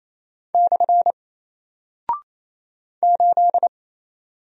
Morse code, heard as a keyed tone, about 27 wpm, 700 Hz. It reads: / 8
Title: Single Character (Letter:Number:Punctuation:Prosign) 28wpm